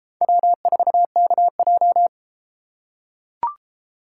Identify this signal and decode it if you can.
Morse code, heard as a keyed tone, about 32 wpm, 700 Hz. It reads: W4KJ